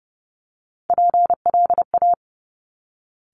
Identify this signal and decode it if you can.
Morse code, heard as a keyed tone, about 30 wpm, 700 Hz. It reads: PLA
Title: Top Three Letter Combinations in Words 30wpm